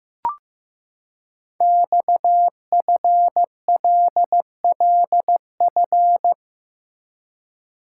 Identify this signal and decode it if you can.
Morse code, heard as a keyed tone, about 15 wpm, 700 Hz. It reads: XFLLF
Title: Code Groups - 5 Letters - 15wpm